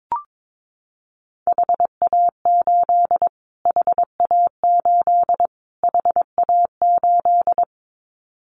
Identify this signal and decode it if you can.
Morse code, heard as a keyed tone, about 22 wpm, 700 Hz. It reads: HA8 HA8 HA8
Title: Callsign Prefixes for Top 25 DXCC - Repeated 3x - 22wpm